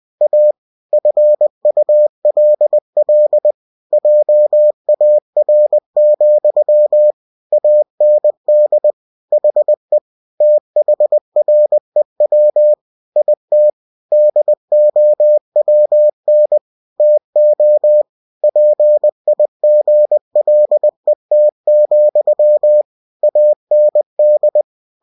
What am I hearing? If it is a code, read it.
Morse code, heard as a keyed tone, about 20 wpm, 600 Hz. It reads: A FULL JAR, AND HE THREW IT DOWN TO PIGLET, AND
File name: cw-books / winnie-the-pooh / winnie-20wpm-page-0038